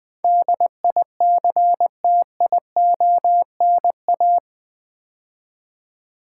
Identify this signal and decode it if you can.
Morse code, heard as a keyed tone, about 20 wpm, 700 Hz. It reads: DICTIONA
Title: Build A Word - 4 Letters At A Time - 10-Letter Words - 20wpm